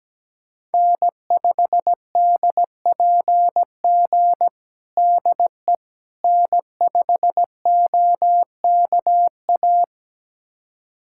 Morse code, heard as a keyed tone, about 17 wpm, 700 Hz. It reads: N5DPG DE N5OKA
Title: USA Callsign DE Callsign - 5 Characters - 17wpm